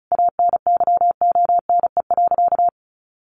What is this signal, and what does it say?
Morse code, heard as a keyed tone, about 34 wpm, 700 Hz. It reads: ANYONE.
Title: Sentences from Top 1200 Words 35wpm